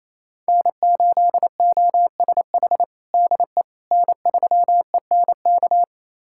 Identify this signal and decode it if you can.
Morse code, heard as a keyed tone, about 27 wpm, 700 Hz. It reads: N8OSH DE N3ENK
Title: USA Callsign DE Callsign - 5 Characters - 28wpm